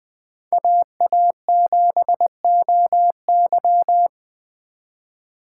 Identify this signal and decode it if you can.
Morse code, heard as a keyed tone, about 21 wpm, 700 Hz. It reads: AA7OY
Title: 2023 SOTA Chaser Call Signs - 20wpm